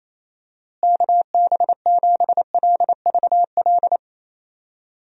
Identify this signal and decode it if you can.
Morse code, heard as a keyed tone, about 28 wpm, 700 Hz. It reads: KB7LVL
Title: USA Call Signs - 6 Characters - Repeated 3x - 28wpm